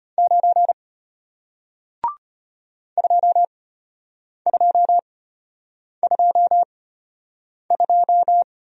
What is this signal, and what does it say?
Morse code, heard as a keyed tone, about 32 wpm, spaced as for 8 wpm, 690 Hz. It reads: 9 2222